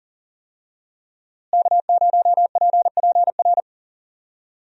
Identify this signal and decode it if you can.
Morse code, heard as a keyed tone, about 41 wpm, 700 Hz. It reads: K0PPR
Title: USA Call Signs - 5 Characters - Repeated 3x - 40wpm